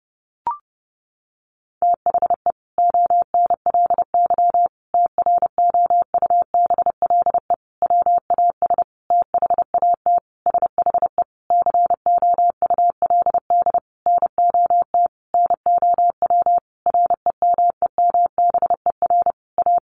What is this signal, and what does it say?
Morse code, heard as a keyed tone, about 30 wpm, 700 Hz. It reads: THE ONLY TROUBLE WAS THAT SHE COULD NOT NOW REMEMBER A
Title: Sentences from Top 1200 Words 30wpm